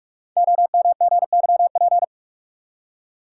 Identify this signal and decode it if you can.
Morse code, heard as a keyed tone, about 44 wpm, 700 Hz. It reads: OMGYP
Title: Code Groups - 5 Letters - 45wpm